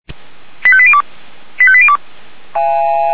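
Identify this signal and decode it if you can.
Five-tone selective call (ZVEI), heard as quick runs of five tones, tones of about 70 ms. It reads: 85802 85802
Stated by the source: ZVEI five-tone selective call received on SDR.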